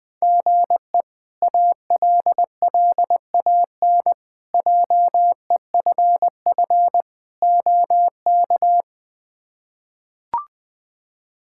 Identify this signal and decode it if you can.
Morse code, heard as a keyed tone, about 20 wpm, 700 Hz. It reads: GE ALLAN JEFF OK
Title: K1USN SST QSO Practice - 20wpm